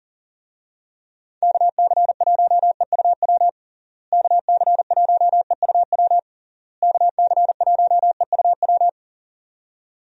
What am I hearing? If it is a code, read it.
Morse code, heard as a keyed tone, about 40 wpm, 700 Hz. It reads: KC1EUW KC1EUW KC1EUW
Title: USA Call Signs - 6 Characters - Repeated 3x - 40wpm